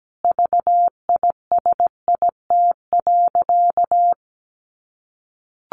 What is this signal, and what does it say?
Morse code, heard as a keyed tone, about 17 wpm, 700 Hz. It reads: VISIT.
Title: Sentences from Top 1400 Words 17wpm